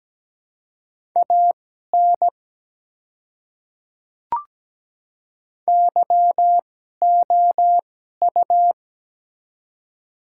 Morse code, heard as a keyed tone, about 17 wpm, spaced as for 12 wpm, 690 Hz. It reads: AN YOU
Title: 2x Character Spacing - Top 100 Words - 17wpm